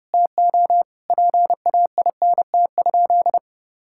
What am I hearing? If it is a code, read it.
Morse code, heard as a keyed tone, about 32 wpm, 700 Hz. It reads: TO PAINT?
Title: Sentences from Top 1000 Words - Review Part 6 - 30wpm